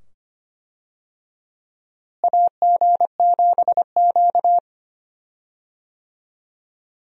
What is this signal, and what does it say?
Morse code, heard as a keyed tone, about 25 wpm, 700 Hz. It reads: AG7Q
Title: USA Call Signs - 4 Characters - Repeated 2x - 25wpm